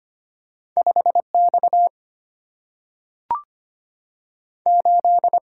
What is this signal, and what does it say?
Morse code, heard as a keyed tone, about 25 wpm, 700 Hz. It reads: 5 X 8
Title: Single-Letter Single-Number 25wpm